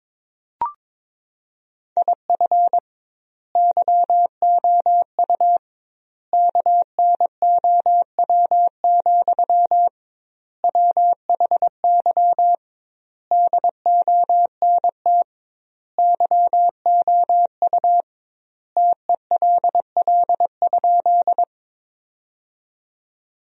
Morse code, heard as a keyed tone, about 22 wpm, 700 Hz. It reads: IF YOU KNOW, WHY DONT YOU TELL?
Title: 2x Word-Spacing - Sentences from Top 300 Words - 22wpm